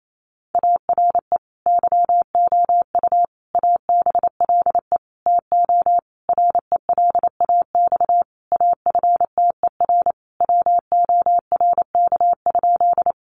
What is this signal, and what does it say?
Morse code, heard as a keyed tone, about 28 wpm, 700 Hz. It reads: ARE YOU ABLE TO RELAX AFTER WORK?
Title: Sentences from Top 1400 Words 28wpm